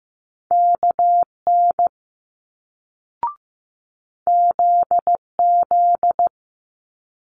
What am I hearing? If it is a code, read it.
Morse code, heard as a keyed tone, about 15 wpm, 700 Hz. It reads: KN ZZ
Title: Multi-letter Phonemes Quick Fire 15wpm